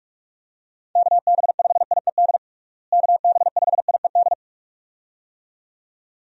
Morse code, heard as a keyed tone, about 41 wpm, 700 Hz. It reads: KB5IED KB5IED
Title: USA Call Signs - 6 Characters - Repeated 2x - 45wpm